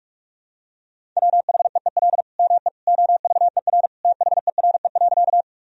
Morse code, heard as a keyed tone, about 41 wpm, 700 Hz. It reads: WHEEL ME OVER THERE.
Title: Sentences from Top 1000 Words - Review Part 3 - 45wpm